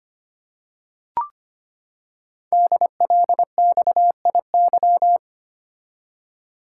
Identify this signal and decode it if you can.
Morse code, heard as a keyed tone, about 25 wpm, 700 Hz. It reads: DLXIY
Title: Code Groups - 5 Letters - 25wpm